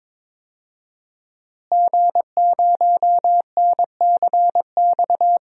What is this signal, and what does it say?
Morse code, heard as a keyed tone, about 21 wpm, 700 Hz. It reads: G0NCX
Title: DXCC-100 Call Signs - 5 Characters - Repeated 3x - 22wpm